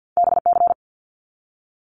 Morse code, heard as a keyed tone, about 50 wpm, 700 Hz. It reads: BC
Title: Canadian Provinces and Territories 50wpm